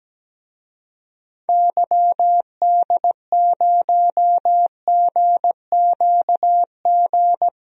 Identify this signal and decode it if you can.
Morse code, heard as a keyed tone, about 17 wpm, 700 Hz. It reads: YD0GQG
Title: DXCC-100 Call Signs - 6 Characters - 17wpm